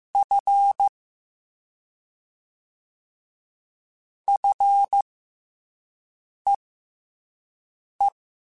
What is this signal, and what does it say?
Morse code, heard as a keyed tone, about 15 wpm, spaced as for 5 wpm, 800 Hz. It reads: F FEE
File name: F and L / F and L mp3 / F and L Prefixes 3x2 Twice @5